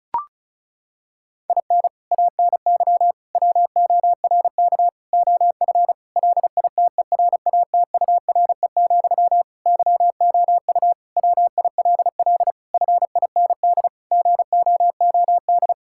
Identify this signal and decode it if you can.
Morse code, heard as a keyed tone, about 36 wpm, 700 Hz. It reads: IN ANY WORK OF LITERATURE, YOU WILL FIND GOOD
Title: Sentences from Top 1000 Words - Review Part 6 - 35wpm